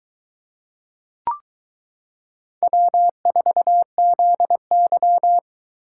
Morse code, heard as a keyed tone, about 22 wpm, 700 Hz. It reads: W4ZY